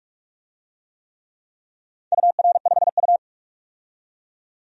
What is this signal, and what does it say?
Morse code, heard as a keyed tone, about 47 wpm, 700 Hz. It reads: UR5U
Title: DXCC-100 Call Signs - 4 Characters - 45wpm